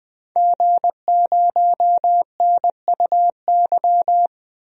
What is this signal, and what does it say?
Morse code, heard as a keyed tone, about 21 wpm, 700 Hz. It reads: G0NUY
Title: DXCC-100 Call Signs - 5 Characters - Repeated 2x - 20wpm